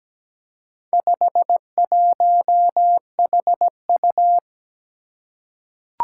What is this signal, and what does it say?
Morse code, heard as a keyed tone, about 17 wpm, 700 Hz. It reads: 51HU
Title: Four Challenging Letter-Number Combinations 17wpm